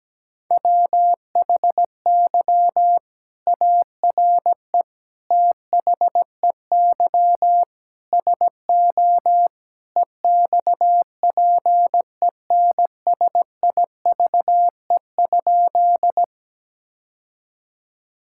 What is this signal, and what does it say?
Morse code, heard as a keyed tone, about 17 wpm, 700 Hz. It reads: WHY ARE THEY SO EXPENSIVE?
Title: Sentences from Top 1000 Words - Review Part 6 - 17wpm